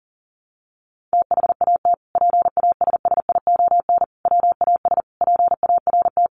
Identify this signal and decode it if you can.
Morse code, heard as a keyed tone, about 40 wpm, 700 Hz. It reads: THAT PASSION WAS PART
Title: Sentences from Top 1300 Words 40wpm